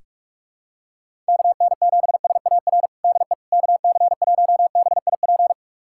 Morse code, heard as a keyed tone, about 46 wpm, 700 Hz. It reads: KN7SAR DE KC1BIP
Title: USA Callsign DE Callsign 45wpm